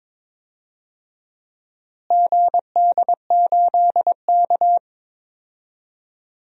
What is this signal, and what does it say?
Morse code, heard as a keyed tone, about 21 wpm, 700 Hz. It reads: GD8K
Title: DXCC-100 Call Signs - 4 Characters - Repeated 3x - 22wpm